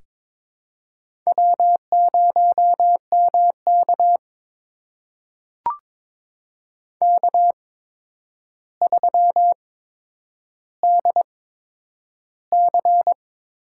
Morse code, heard as a keyed tone, about 22 wpm, spaced as for 9 wpm, 700 Hz. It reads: W0MK K3DC
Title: USA Call Signs - 4 Characters - 8x Character Spacing - 22wpm